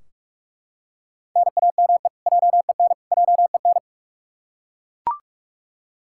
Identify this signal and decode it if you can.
Morse code, heard as a keyed tone, about 48 wpm, 700 Hz. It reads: NAME JEN JEN
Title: K1USN SST QSO Practice 45wpm